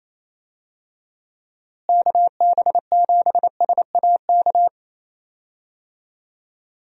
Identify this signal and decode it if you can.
Morse code, heard as a keyed tone, about 28 wpm, 700 Hz. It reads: KB7SAK
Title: USA Call Signs - 6 Characters - Repeated 2x - 28wpm